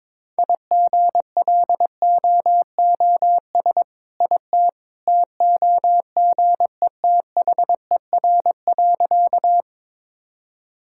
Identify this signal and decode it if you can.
Morse code, heard as a keyed tone, about 22 wpm, 690 Hz. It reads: IGLOOS IT TOGETHER.
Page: morse-code-ninja-practice-22wpm